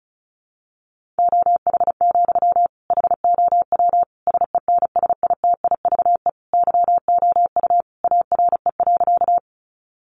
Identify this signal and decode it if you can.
Morse code, heard as a keyed tone, about 34 wpm, 700 Hz. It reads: OH, HOW SENSITIVE YOU ARE.
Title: Sentences from Top 1400 Words 35wpm